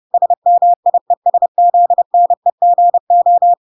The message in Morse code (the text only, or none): SMIESZNEGO